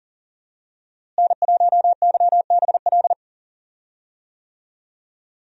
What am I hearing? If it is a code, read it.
Morse code, heard as a keyed tone, about 40 wpm, 700 Hz. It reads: N1YBL
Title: USA Call Signs - 5 Characters - Repeated 2x - 40wpm